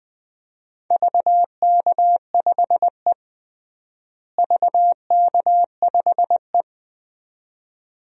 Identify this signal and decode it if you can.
Morse code, heard as a keyed tone, about 20 wpm, 700 Hz. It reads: VK5E VK5E
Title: DXCC-100 Call Signs - 4 Characters - Repeated 2x - 20wpm